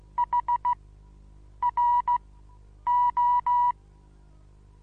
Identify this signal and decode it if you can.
Morse code, heard as a keyed tone, about 15 wpm, spaced as for 8 wpm, 970 Hz. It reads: HRO
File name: Onlinelehrgang / morsen / DJ4UF Morsen25